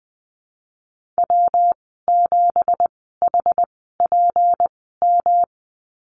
Morse code, heard as a keyed tone, about 20 wpm, spaced as for 15 wpm, 700 Hz. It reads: W7HPM
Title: 167 Lesson 27g - Number 7 - Call Signs 20wpm